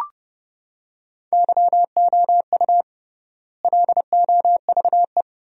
Morse code, heard as a keyed tone, about 31 wpm, 700 Hz. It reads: YOU LOVE
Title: 3x Word-Spacing - Sentences from Top 500 Words - 30wpm